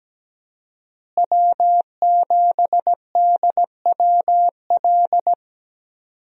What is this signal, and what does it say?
Morse code, heard as a keyed tone, about 17 wpm, 700 Hz. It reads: W7DWL
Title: USA Call Signs - 5 Characters - Repeated 2x - 17wpm